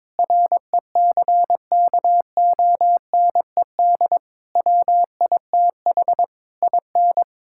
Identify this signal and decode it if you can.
Morse code, heard as a keyed tone, about 21 wpm, 700 Hz. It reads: RECKONED WITH IN
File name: practice / 22wpm / Sets of 8 Words - 22wpm